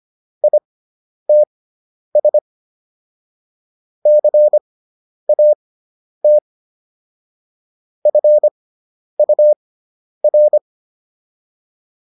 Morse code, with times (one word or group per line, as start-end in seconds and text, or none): 0.4-2.4 ITS
4.1-6.4 CAT
8.1-10.6 FUR